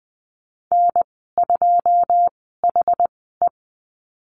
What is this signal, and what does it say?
Morse code, heard as a keyed tone, about 20 wpm, spaced as for 15 wpm, 700 Hz. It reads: N2HE
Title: 147 Lesson 24g - Letter y - Call Signs 20wpm